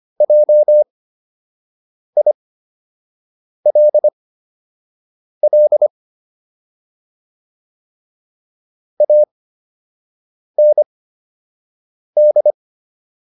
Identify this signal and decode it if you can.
Morse code, heard as a keyed tone, about 25 wpm, spaced as for 6 wpm, 600 Hz. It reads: JILL AND